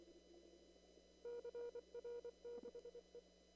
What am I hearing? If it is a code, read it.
Morse code, heard as a keyed tone, about 24 wpm, 480 Hz. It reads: CR6E
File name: CW / kt5j wpx cw-op-k5pi